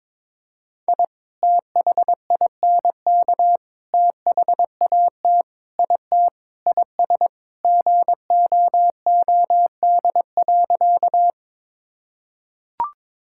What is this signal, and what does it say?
Morse code, heard as a keyed tone, about 22 wpm, 700 Hz. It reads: I THINK THAT IT IS GOOD.
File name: practice / 22wpm / Sentences with 3-Word Phrases - 22wpm